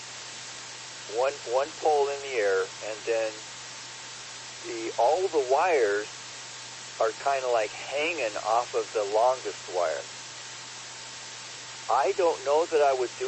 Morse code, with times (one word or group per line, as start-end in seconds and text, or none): none